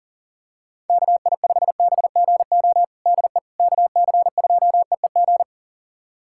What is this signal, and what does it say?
Morse code, heard as a keyed tone, about 40 wpm, 700 Hz. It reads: KI5BGO DE KC2EEG
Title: USA Callsign DE Callsign 40wpm